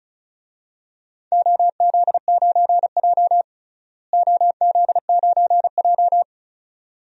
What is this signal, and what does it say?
Morse code, heard as a keyed tone, about 34 wpm, 700 Hz. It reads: OZ9J OZ9J
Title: DXCC-100 Call Signs - 4 Characters - Repeated 2x - 35wpm